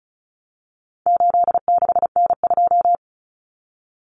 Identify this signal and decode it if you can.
Morse code, heard as a keyed tone, about 34 wpm, 700 Hz. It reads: 86N2
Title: Common Cut Numbers - 4-Digit 35wpm